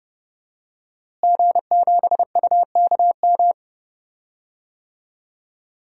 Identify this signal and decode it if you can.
Morse code, heard as a keyed tone, about 30 wpm, 700 Hz. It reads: G7UKM
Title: DXCC-100 Call Signs - 5 Characters - Repeated 2x - 30wpm